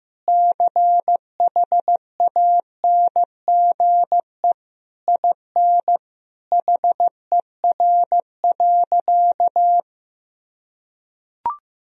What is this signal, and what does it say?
Morse code, heard as a keyed tone, about 15 wpm, 700 Hz. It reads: CHANGE IN HER.
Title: Sentences from Top 1000 Words - Review Part 1 - 15wpm